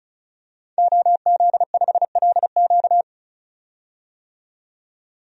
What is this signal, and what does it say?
Morse code, heard as a keyed tone, about 36 wpm, 700 Hz. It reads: OZ5LQ
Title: DXCC-100 Call Signs - 5 Characters - 35wpm